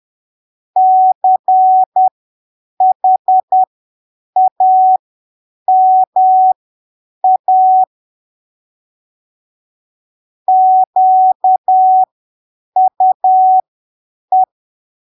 Morse code, heard as a keyed tone, about 10 wpm, spaced as for 7 wpm, 750 Hz. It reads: CHAMA QUE
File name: arquivos / texto / texto 29 10ppm